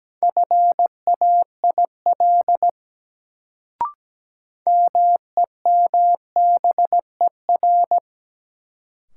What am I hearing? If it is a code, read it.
Morse code, heard as a keyed tone, about 17 wpm, 700 Hz. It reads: FAIL MEMBER
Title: Sentences from Top 1100 Words - New Word Drill - 17wpm